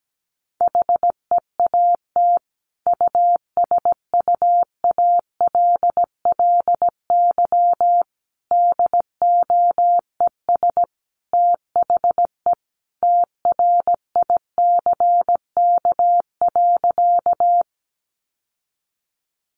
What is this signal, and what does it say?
Morse code, heard as a keyed tone, about 17 wpm, 700 Hz. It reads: HEAT USUALLY DOES THE TRICK.
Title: Sentences from Top 1500 Words 17wpm